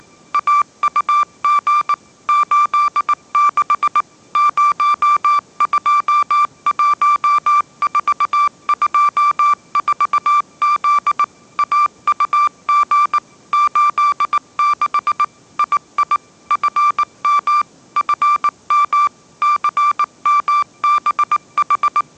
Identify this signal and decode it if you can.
Morse code, heard as a keyed tone, about 19 wpm, 1180 Hz. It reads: AUG 86 021424Z AUG 86 II FM FM CMBH
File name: mrx083 20wpm